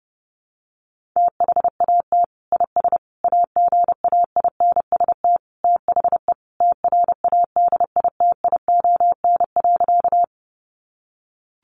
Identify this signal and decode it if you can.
Morse code, heard as a keyed tone, about 30 wpm, 700 Hz. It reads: THAT IS AGAINST THE TRADITION.
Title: Sentences from Top 1500 Words 30wpm